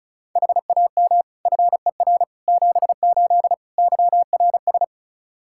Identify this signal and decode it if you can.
Morse code, heard as a keyed tone, about 34 wpm, 700 Hz. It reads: HAM FER 78 YRS
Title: Practice QSO - Exchange 3 - More Info - 35wpm